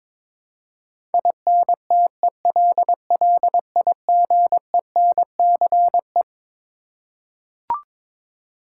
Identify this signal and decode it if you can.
Morse code, heard as a keyed tone, about 21 wpm, 700 Hz. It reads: INTELLIGENCE
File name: Twelve Letter Words - 22wpm